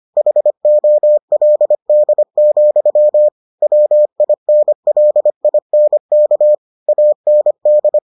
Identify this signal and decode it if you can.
Morse code, heard as a keyed tone, about 25 wpm, 600 Hz. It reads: HOLD, WINLINK AND